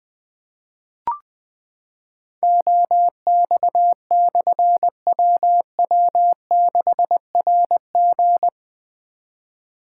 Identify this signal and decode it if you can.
Morse code, heard as a keyed tone, about 20 wpm, 700 Hz. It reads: OX/WW6RG
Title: Difficult Call Signs 20wpm